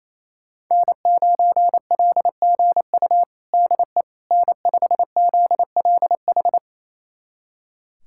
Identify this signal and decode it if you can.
Morse code, heard as a keyed tone, about 28 wpm, 700 Hz. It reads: N9LGU DE N5ZLH
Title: USA Callsign DE Callsign - 5 Characters - 28wpm